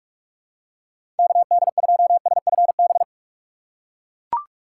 Morse code, heard as a keyed tone, about 44 wpm, 700 Hz. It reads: KD2SFB
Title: USA Call Signs - 6 Characters - 6x Character Spacing - 45wpm